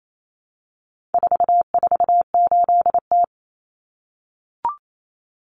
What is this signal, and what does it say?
Morse code, heard as a keyed tone, about 28 wpm, 700 Hz. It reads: 448T
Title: Common Cut Numbers - 4-Digit 28wpm